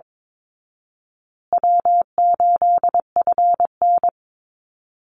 Morse code, heard as a keyed tone, about 22 wpm, 700 Hz. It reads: W8FN
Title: Easy Call Signs Repeated 3x - 22wpm